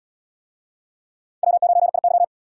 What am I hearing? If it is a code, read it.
Morse code, heard as a keyed tone, about 77 wpm, 700 Hz. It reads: W9IY